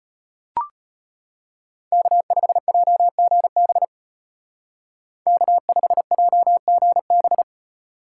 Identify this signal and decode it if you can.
Morse code, heard as a keyed tone, about 31 wpm, 700 Hz. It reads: K5JGB K5JGB
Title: Speed Racing - USA 1x3 Call Signs 25wpm